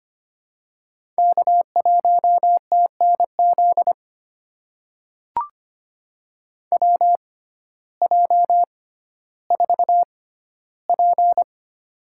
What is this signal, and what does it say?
Morse code, heard as a keyed tone, about 25 wpm, spaced as for 14 wpm, 700 Hz. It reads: K1TNZ WJ4P
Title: USA Call Signs - 6x Character Spacing - 25wpm